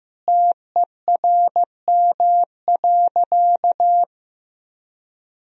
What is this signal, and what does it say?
Morse code, heard as a keyed tone, about 15 wpm, 700 Hz. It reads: TERM.
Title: Sentences from Top 1000 Words - Review Part 4 - 15wpm